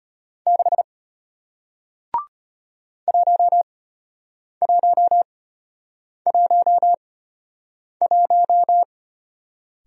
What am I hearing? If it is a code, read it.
Morse code, heard as a keyed tone, about 34 wpm, spaced as for 8 wpm, 700 Hz. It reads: 6 1111